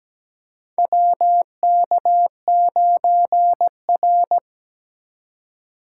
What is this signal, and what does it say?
Morse code, heard as a keyed tone, about 17 wpm, 700 Hz. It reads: WK9R
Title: USA Call Signs - 6x Character Spacing - 17wpm